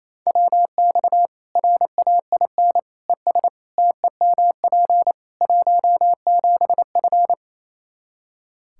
Morse code, heard as a keyed tone, about 28 wpm, 700 Hz. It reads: WX RAIN ES TEMP 17F
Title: Practice QSO - Exchange 2 - Info - 28wpm